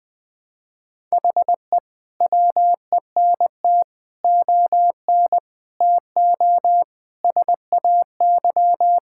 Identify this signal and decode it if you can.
Morse code, heard as a keyed tone, about 20 wpm, 700 Hz. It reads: HE WENT ON TO SAY
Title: Sentences with 6-Word Phrases - 20wpm